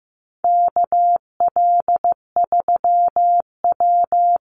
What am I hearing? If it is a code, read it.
Morse code, heard as a keyed tone, about 15 wpm, 700 Hz. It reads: KL3W
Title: Call Signs Repeated 2x - 15wpm